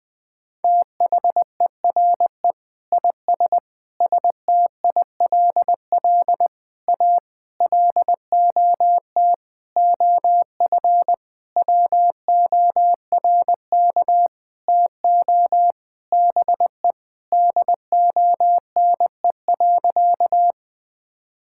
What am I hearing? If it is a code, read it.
Morse code, heard as a keyed tone, about 20 wpm, 700 Hz. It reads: THERE IS STILL A LOT OF WORK TO BE DONE.
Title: Sentences with 9-Word Phrases - 20wpm